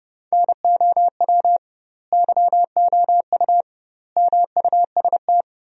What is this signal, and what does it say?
Morse code, heard as a keyed tone, about 32 wpm, 700 Hz. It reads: NOW YOU MUST
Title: 2x Word-Spacing - Sentences from Top 300 Words - 30wpm